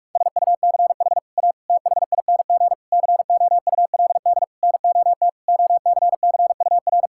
Morse code, heard as a keyed tone, about 41 wpm, 700 Hz. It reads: SUCH A THING COULD NOT OCCUR
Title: Sentences from Top 1000 Words - Review Part 6 - 45wpm